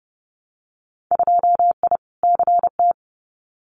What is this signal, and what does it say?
Morse code, heard as a keyed tone, about 30 wpm, 700 Hz. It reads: 2I CT
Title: Winter Field Day Exchanges 30wpm